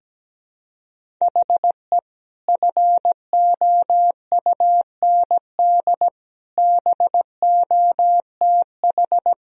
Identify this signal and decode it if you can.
Morse code, heard as a keyed tone, about 17 wpm, 700 Hz. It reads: HE FOUND BOTH
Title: Sentences from Top 1000 Words - Review Part 1 - 17wpm